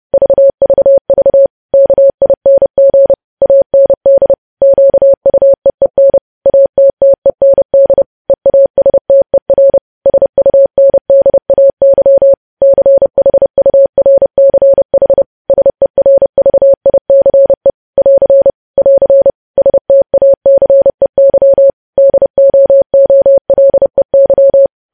Morse code, heard as a keyed tone, about 30 wpm, 550 Hz. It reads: VVV KING AND QUEEN ATTEND EASTER SUNDAY CHURCH SERVICE <AR> <AR> STACEY DOOLEY
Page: news headlines in morse code at 30 wpm